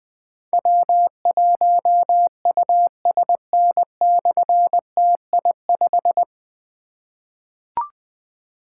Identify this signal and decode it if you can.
Morse code, heard as a keyed tone, about 21 wpm, 700 Hz. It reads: W1USN/TI5
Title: Difficult Call Signs 20wpm